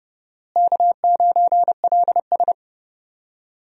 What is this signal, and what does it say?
Morse code, heard as a keyed tone, about 30 wpm, 700 Hz. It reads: K9LS
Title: USA Call Signs - 4 Characters - Repeated 2x - 30wpm